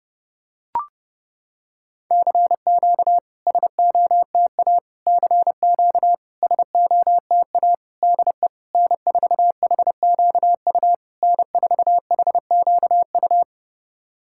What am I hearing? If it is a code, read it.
Morse code, heard as a keyed tone, about 30 wpm, 690 Hz. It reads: CQ SOTA CQ SOTA DE N4HQU N4HQU
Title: SOTA QSO Practice 30wpm